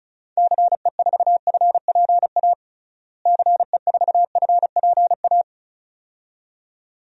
Morse code, heard as a keyed tone, about 34 wpm, 700 Hz. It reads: CE4FPA CE4FPA
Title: DXCC-100 Call Signs - 6 Characters - Repeated 2x - 35wpm